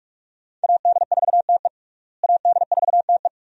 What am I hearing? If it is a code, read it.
Morse code, heard as a keyed tone, about 46 wpm, 700 Hz. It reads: AD4TE AD4TE
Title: USA Call Signs - 5 Characters - Repeated 2x - 45wpm